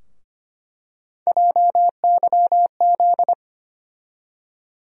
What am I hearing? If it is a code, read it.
Morse code, heard as a keyed tone, about 25 wpm, 700 Hz. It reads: JYZ
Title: USA Call Sign Postfixes - 3 Letters 25wpm